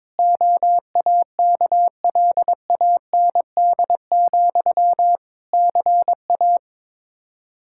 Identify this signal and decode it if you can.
Morse code, heard as a keyed tone, about 21 wpm, 700 Hz. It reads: OAKLAND, CA
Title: K1USN SST QSO Practice 22wpm